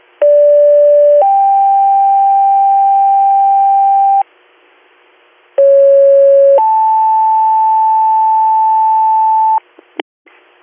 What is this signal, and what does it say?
Two-tone page, heard as a one-second tone, then a three-second tone, about 580 then 790 Hz, about 550 then 880 Hz. It is heard from 0.2 to 9.6 s.